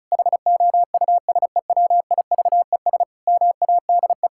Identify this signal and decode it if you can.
Morse code, heard as a keyed tone, about 36 wpm, 700 Hz. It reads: HOUSEWIVES MADE
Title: Pangram Challenge v2 - 35wpm